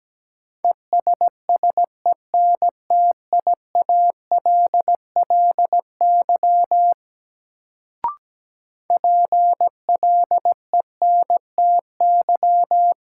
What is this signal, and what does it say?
Morse code, heard as a keyed tone, about 17 wpm, 700 Hz. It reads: ESSENTIALLY PLENTY
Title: Sentences from Top 1300 Words - New Word Drill Alt - 17wpm